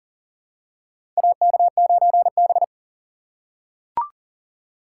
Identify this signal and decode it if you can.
Morse code, heard as a keyed tone, about 40 wpm, 700 Hz. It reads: AK9B
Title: USA Call Signs - 4 Characters - 2x Character Spacing - 40wpm